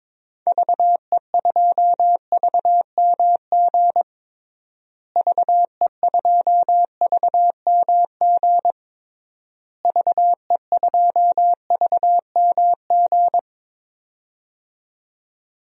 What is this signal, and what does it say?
Morse code, heard as a keyed tone, about 22 wpm, 700 Hz. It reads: VE2VMG VE2VMG VE2VMG
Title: DXCC-100 Call Signs - 6 Characters - Repeated 3x - 22wpm